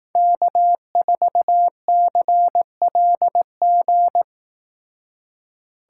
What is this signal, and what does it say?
Morse code, heard as a keyed tone, about 18 wpm, 700 Hz. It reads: K4CLG
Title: Speed Racing - USA Call Signs 15wpm